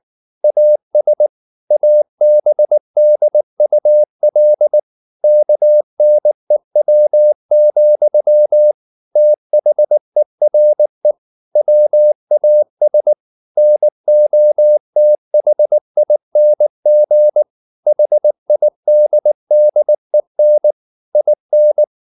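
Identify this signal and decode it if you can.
Morse code, heard as a keyed tone, about 19 wpm, 600 Hz. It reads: AS ABDUL KNEW, THERE WAS NOTHING HIDDEN IN
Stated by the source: Morse code at 19 wpm from CW med Gnister, reading Ken Follett's novel Never in English.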